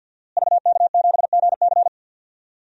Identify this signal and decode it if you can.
Morse code, heard as a keyed tone, about 46 wpm, 700 Hz. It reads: VK7GC
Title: DXCC-100 Call Signs - 5 Characters - Repeated 3x - 50wpm